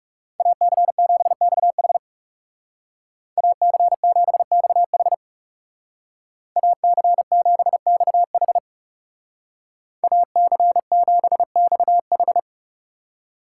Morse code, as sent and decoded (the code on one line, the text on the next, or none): .- -.-. --... -..- .... / .- -.-. --... -..- .... / .- -.-. --... -..- .... / .- -.-. --... -..- ....
AC7XH AC7XH AC7XH AC7XH